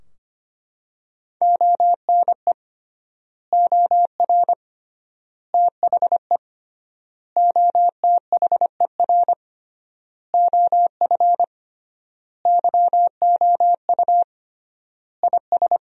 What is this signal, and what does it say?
Morse code, heard as a keyed tone, about 25 wpm, 700 Hz. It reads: ONE OR THE OTHER OF YOU IS
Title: 3x Word-Spacing - Sentences from Top 400 Words - 25wpm